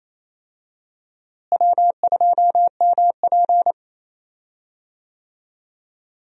Morse code, heard as a keyed tone, about 28 wpm, 700 Hz. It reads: W2MP
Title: USA Call Signs - 4 Characters - Repeated 3x - 28wpm